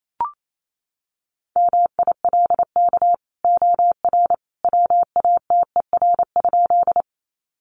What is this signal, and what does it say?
Morse code, heard as a keyed tone, about 28 wpm, 700 Hz. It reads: MILK OR WATER?
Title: Sentences from Top 1500 Words 28wpm